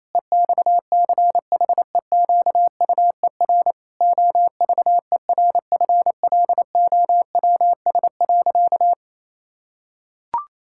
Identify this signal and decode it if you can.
Morse code, heard as a keyed tone, about 28 wpm, 700 Hz. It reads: EXCHEQUER OVERFLOWS.
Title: Pangram Challenge v2 - 28wpm